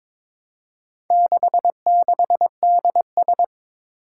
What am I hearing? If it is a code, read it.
Morse code, heard as a keyed tone, about 22 wpm, 700 Hz. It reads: 66DS
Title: Four Challenging Letter-Number Combinations 22wpm